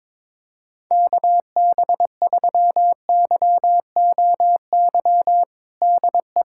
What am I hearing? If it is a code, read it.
Morse code, heard as a keyed tone, about 22 wpm, 700 Hz. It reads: KB3YOY DE
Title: USA Callsign DE Callsign 22wpm